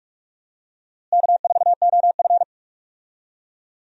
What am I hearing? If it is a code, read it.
Morse code, heard as a keyed tone, about 46 wpm, 700 Hz. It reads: K4OF
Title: USA Call Signs - 4 Characters - Repeated 2x - 45wpm